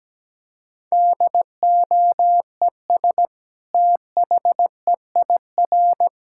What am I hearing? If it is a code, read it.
Morse code, heard as a keyed tone, about 17 wpm, 700 Hz. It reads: DOES THEIR
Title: Sentences from Top 1000 Words - Review Part 5 - 17wpm